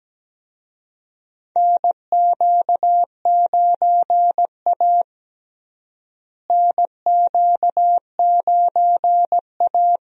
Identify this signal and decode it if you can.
Morse code, heard as a keyed tone, about 17 wpm, 700 Hz. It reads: NQ9A NQ9A
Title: USA Call Signs - 4 Characters - Repeated 2x - 17wpm